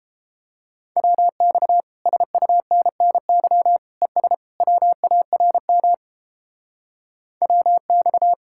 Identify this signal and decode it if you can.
Morse code, heard as a keyed tone, about 31 wpm, 700 Hz. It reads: WX SUNNY ES WARM WX